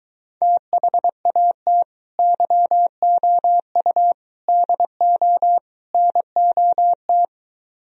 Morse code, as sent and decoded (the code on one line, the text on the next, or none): - .... .- - / -.-- --- ..- / -.. --- / -. --- -
THAT YOU DO NOT